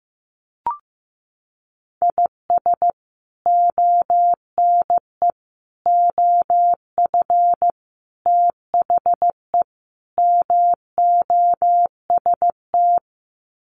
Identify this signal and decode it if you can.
Morse code, heard as a keyed tone, about 15 wpm, 700 Hz. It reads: IS ONE OF THE MOST
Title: Sets of 5 Words 15wpm